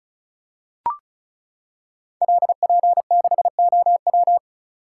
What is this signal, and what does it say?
Morse code, heard as a keyed tone, about 36 wpm, 700 Hz. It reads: LP6OW
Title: Code Groups - 5 Letter:Numbers - 35wpm